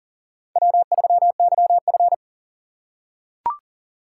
Morse code, heard as a keyed tone, about 40 wpm, 700 Hz. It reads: W3YF
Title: USA Call Signs - 4 Characters - 6x Character Spacing - 40wpm